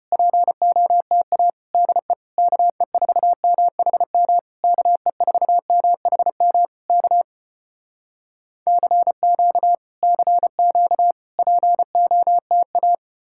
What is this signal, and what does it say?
Morse code, heard as a keyed tone, about 34 wpm, 700 Hz. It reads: POTA DE KE4MHM KE4MHM K CQ CQ POTA